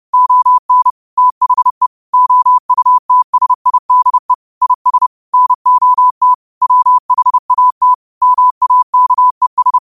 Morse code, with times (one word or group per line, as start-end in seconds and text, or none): 0.1-0.9 ON
1.2-1.9 THE
2.1-4.3 OUTSIDE
4.6-5.1 IS
5.3-6.3 NOT
6.6-7.9 WHAT
8.2-9.8 MAKES